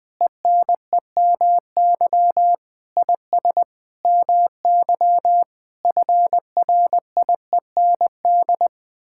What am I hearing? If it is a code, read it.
Morse code, heard as a keyed tone, about 20 wpm, 700 Hz. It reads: ENEMY IS MY FRIEND
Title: Sets of 8 Words - 20wpm